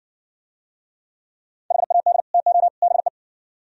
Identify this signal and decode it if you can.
Morse code, heard as a keyed tone, about 56 wpm, 700 Hz. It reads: HAD TO BE